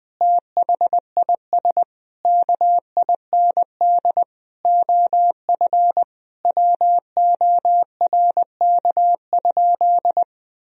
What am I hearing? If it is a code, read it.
Morse code, heard as a keyed tone, about 19 wpm, 700 Hz. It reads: THIS KIND OF WORK?
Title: Sentences from Top 1000 Words - Review Part 3 - 20wpm